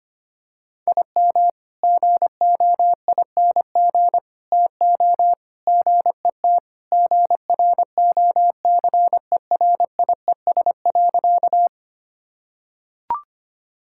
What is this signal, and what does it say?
Morse code, heard as a keyed tone, about 25 wpm, 700 Hz. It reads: IM GOING TO GET GROCERIES.
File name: practice / 25wpm / Sentences with 3-Word Phrases - 25wpm